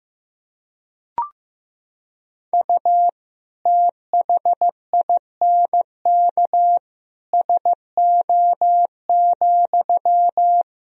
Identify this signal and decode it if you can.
Morse code, heard as a keyed tone, about 15 wpm, 700 Hz. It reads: U THINK SO,
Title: Sentences from Top 200 Words and Abbreviations 15wpm